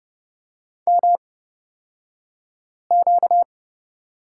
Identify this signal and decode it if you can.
Morse code, heard as a keyed tone, about 30 wpm, spaced as for 5 wpm, 700 Hz. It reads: MQ
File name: Lesson 38d - Letter q - Two Characters 30wpm